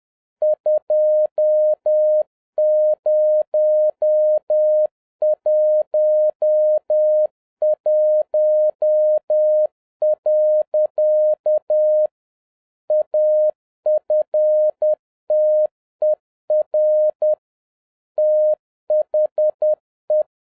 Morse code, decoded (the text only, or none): 2011. AFTER THE